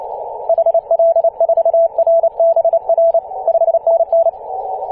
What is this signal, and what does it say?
Morse code, heard as a keyed tone, about 28 wpm, 670 Hz. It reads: HL4RBR 5NN
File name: ep2a-21-029mhz-cw